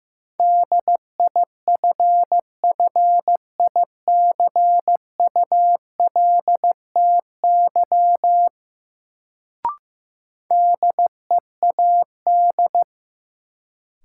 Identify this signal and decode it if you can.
Morse code, heard as a keyed tone, about 16 wpm, 700 Hz. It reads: DIFFICULTY DEAD
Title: Sentences from Top 1500 Words - New Word Drill Alt - 15wpm